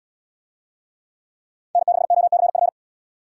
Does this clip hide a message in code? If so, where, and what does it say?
Yes: Morse code from 1.8 to 2.7 s, heard as a keyed tone, about 71 wpm, 700 Hz. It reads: N6OCF